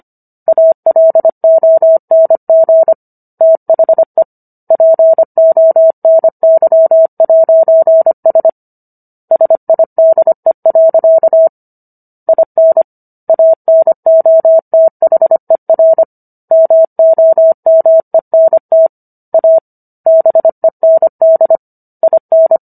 Morse code, heard as a keyed tone, about 25 wpm, 650 Hz. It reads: ALONG THE PONY'S SIDE. IN ANOTHER MOMENT A BEND IN